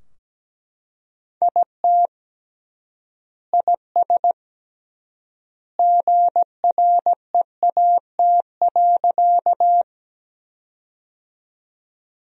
Morse code, heard as a keyed tone, about 17 wpm, 700 Hz. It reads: IT IS GREAT.
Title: 3x Word-Spacing - Sentences from Top 200 Words - 17wpm